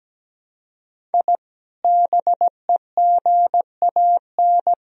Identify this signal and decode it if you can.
Morse code, heard as a keyed tone, about 17 wpm, 700 Hz. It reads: I BEGAN
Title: Sentences from Top 1000 Words - Review Part 2 - 17wpm